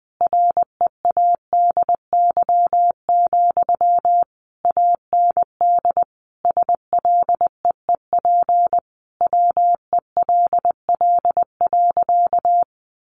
Morse code, heard as a keyed tone, about 19 wpm, 700 Hz. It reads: READY, AND SLEEP WELL.
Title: Sentences from Top 500 Words 20wpm